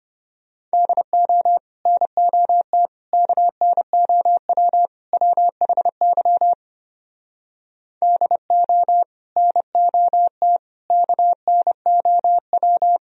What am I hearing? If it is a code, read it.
Morse code, heard as a keyed tone, about 33 wpm, 700 Hz. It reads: DO NOT KNOW WHY DO NOT KNOW